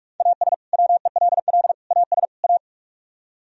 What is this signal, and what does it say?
Morse code, heard as a keyed tone, about 46 wpm, 690 Hz. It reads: AS WELL AS A